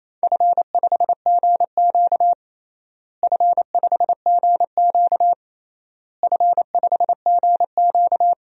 Morse code, heard as a keyed tone, about 28 wpm, 700 Hz. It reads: F5GQ F5GQ F5GQ
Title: DXCC-100 Call Signs - 4 Characters - Repeated 3x - 28wpm